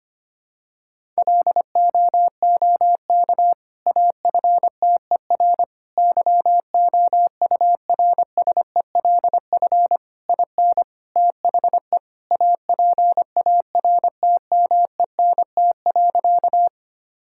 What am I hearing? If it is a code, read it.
Morse code, heard as a keyed tone, about 25 wpm, 700 Hz. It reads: LOOK AFTER YOURSELF IN THE APARTMENT.
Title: Sentences from Top 1000 Words - Review Part 6 - 25wpm